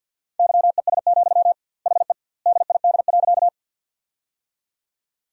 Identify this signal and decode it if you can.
Morse code, heard as a keyed tone, about 50 wpm, 700 Hz. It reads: YES, HE DID.
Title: 2x Word-Spacing - Sentences from Top 500 Words - 50wpm